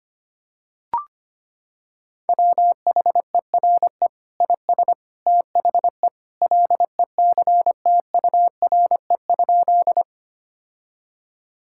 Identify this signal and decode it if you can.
Morse code, heard as a keyed tone, about 25 wpm, 700 Hz. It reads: WHERE IS THE LECTURE?
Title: Sentences from Top 1600 Words 25wpm